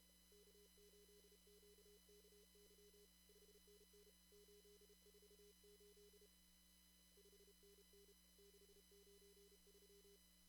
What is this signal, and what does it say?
Morse code, heard as a keyed tone, about 30 wpm, 410 Hz, with noise. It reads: K8QKY 5NN 848 5NN 693
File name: CW / kt5j wpx cw-op-k5pi